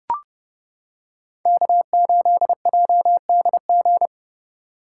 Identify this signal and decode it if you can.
Morse code, heard as a keyed tone, about 29 wpm, 700 Hz. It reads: K8JDG